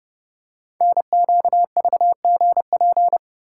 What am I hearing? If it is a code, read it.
Morse code, heard as a keyed tone, about 30 wpm, 700 Hz. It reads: NQVGP
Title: Code Groups - 5 Letters - 30wpm